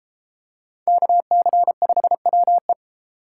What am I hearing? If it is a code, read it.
Morse code, heard as a keyed tone, about 31 wpm, 700 Hz. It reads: KC5WE